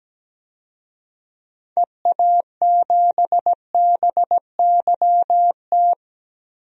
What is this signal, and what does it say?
Morse code, heard as a keyed tone, about 17 wpm, 700 Hz. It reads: EA7BYT
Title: DXCC-100 Call Signs - 6 Characters - 17wpm